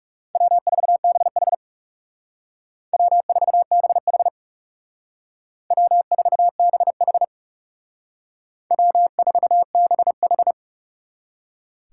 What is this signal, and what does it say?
Morse code, heard as a keyed tone, about 38 wpm, 700 Hz. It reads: W4BH W4BH W4BH W4BH